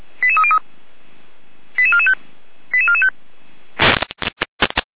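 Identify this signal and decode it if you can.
Five-tone selective call (ZVEI), heard as quick runs of five tones, tones of about 70 ms. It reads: 88303 88445 88445
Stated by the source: Five-tone ZVEI selective call 88265 received by SDR, from the Feuerwehr Orlamünde audio server.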